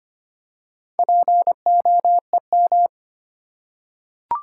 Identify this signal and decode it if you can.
Morse code, heard as a keyed tone, about 25 wpm, 700 Hz. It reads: POEM
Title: Four Letter Words - 25wpm